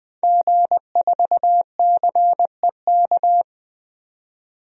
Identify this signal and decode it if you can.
Morse code, heard as a keyed tone, about 20 wpm, 700 Hz. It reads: G4CEK
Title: DXCC-100 Call Signs - 5 Characters - 20wpm